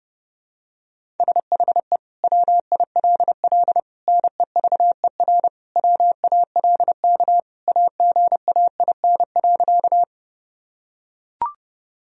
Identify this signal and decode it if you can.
Morse code, heard as a keyed tone, about 30 wpm, 700 Hz. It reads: SHE WILL NEVER WALK AGAIN.
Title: Sentences from Top 1000 Words - Review Part 1 - 30wpm